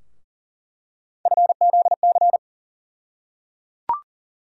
Morse code, heard as a keyed tone, about 37 wpm, 700 Hz. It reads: FZC
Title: USA Call Sign Postfixes - 3 Letters 40wpm